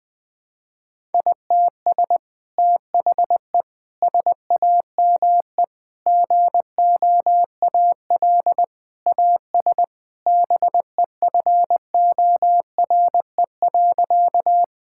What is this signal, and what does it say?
Morse code, heard as a keyed tone, about 21 wpm, 700 Hz. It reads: ITS THE SAME GOAL AS BEFORE.
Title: Sentences from Top 1000 Words - Review Part 5 - 20wpm